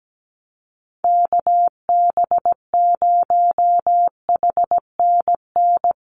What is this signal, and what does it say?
Morse code, heard as a keyed tone, about 17 wpm, 700 Hz. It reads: KB0HNN
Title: USA Call Signs Repeated 2x 17wpm